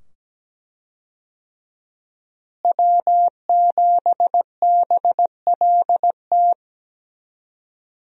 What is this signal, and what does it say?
Morse code, heard as a keyed tone, about 17 wpm, 700 Hz. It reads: W7BLT
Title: USA Call Signs - 5 Characters - Repeated 2x - 17wpm